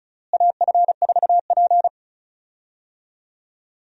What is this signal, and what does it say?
Morse code, heard as a keyed tone, about 36 wpm, 700 Hz. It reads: AF4P
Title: USA Call Signs - 4 Characters - Repeated 3x - 35wpm